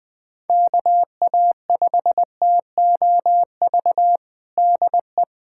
Morse code, heard as a keyed tone, about 20 wpm, 700 Hz. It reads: KA5TOV DE
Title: USA Callsign DE Callsign - 6 Characters - 20wpm